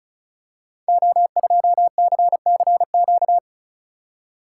Morse code, heard as a keyed tone, about 36 wpm, 690 Hz. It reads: O2CCQ
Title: Code Groups - 5 Letter:Numbers - 35wpm